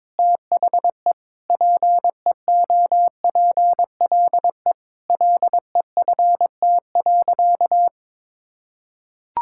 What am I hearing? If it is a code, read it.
Morse code, heard as a keyed tone, about 22 wpm, 700 Hz. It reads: THE PEOPLE LEFT.
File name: practice / 22wpm / Sentences with 4-Word Phrases - 22wpm